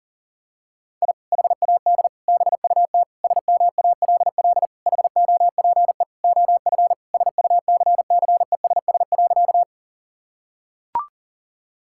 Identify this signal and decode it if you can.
Morse code, heard as a keyed tone, about 40 wpm, 700 Hz. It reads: I HAD BUT SMALL HOPE OF SUCCESS.
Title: Sentences from Top 1000 Words - Review Part 6 - 40wpm